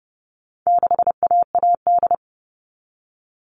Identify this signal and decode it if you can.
Morse code, heard as a keyed tone, about 30 wpm, 700 Hz. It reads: 6AAD
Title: All Cut Numbers - 4-Digit 30wpm